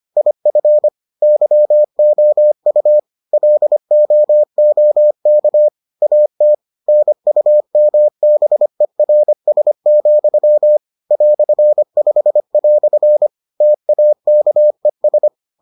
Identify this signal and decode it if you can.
Morse code, heard as a keyed tone, about 25 wpm, 600 Hz. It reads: IF YOU LOOK AT NUMBERS, "5" TAKES